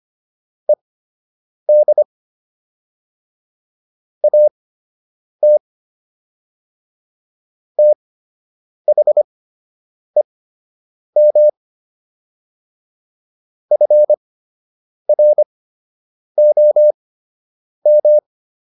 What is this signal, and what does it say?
Morse code, heard as a keyed tone, about 25 wpm, spaced as for 8 wpm, 600 Hz. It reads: ED AT THEM FROM